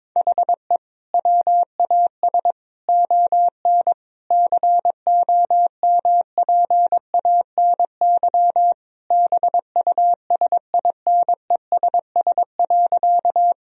Morse code, heard as a keyed tone, about 22 wpm, 700 Hz. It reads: HE WAS ON COMPANY BUSINESS.
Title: Sentences from Top 1000 Words - Review Part 3 - 22wpm